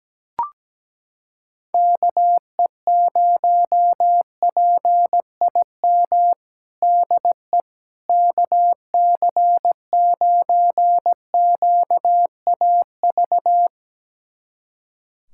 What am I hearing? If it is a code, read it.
Morse code, heard as a keyed tone, about 17 wpm, 700 Hz. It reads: KE0PIM DE KC9QAV
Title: USA Callsign DE Callsign - 6 Characters - 17wpm